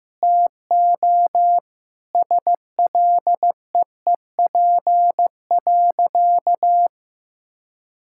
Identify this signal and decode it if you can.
Morse code, heard as a keyed tone, about 15 wpm, 700 Hz. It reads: TO SLEEP.
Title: Sentences from Top 1600 Words 15wpm